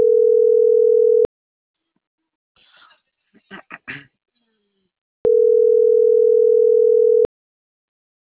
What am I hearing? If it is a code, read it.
Ringback tone (North America), 440 + 480 Hz.